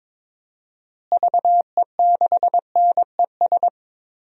Morse code, heard as a keyed tone, about 22 wpm, 700 Hz. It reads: VE6NES
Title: DXCC-100 Call Signs - 6 Characters - Repeated 3x - 22wpm